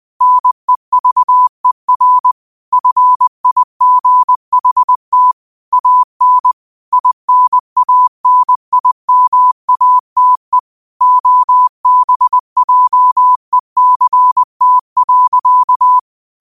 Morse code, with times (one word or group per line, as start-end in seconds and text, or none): 0.2-2.3 NEVER
2.7-5.3 FIGHT
5.7-6.5 AN
6.9-10.6 INANIMATE
11.0-16.0 OBJECT.